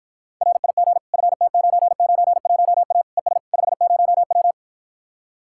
Morse code, heard as a keyed tone, about 56 wpm, 700 Hz. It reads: RIG FT991A ES 50W